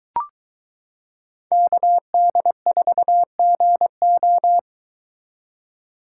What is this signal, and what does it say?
Morse code, heard as a keyed tone, about 23 wpm, 700 Hz. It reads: KD4GO